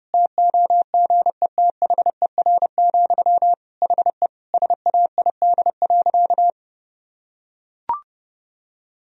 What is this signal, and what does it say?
Morse code, heard as a keyed tone, about 30 wpm, 700 Hz. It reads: TOGETHER, HE SAID.
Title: Sentences from Top 1000 Words - Review Part 2 - 30wpm